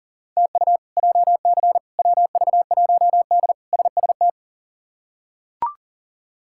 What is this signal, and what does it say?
Morse code, heard as a keyed tone, about 40 wpm, 700 Hz. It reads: TU JC WV1D SST
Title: K1USN SST QSO Practice - 40wpm